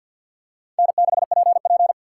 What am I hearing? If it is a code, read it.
Morse code, heard as a keyed tone, about 47 wpm, 700 Hz. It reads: N6PP
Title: USA Call Signs - 4x Character Spacing - 50wpm